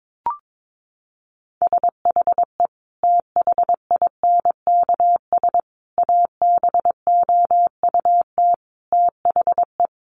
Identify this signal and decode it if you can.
Morse code, heard as a keyed tone, about 22 wpm, 700 Hz. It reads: SHE THINKS ABOUT THE
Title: Easy Phrases 22wpm